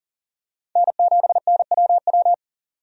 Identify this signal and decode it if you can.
Morse code, heard as a keyed tone, about 40 wpm, 700 Hz. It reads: N7NWW
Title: USA Call Signs - 5 Characters - 40wpm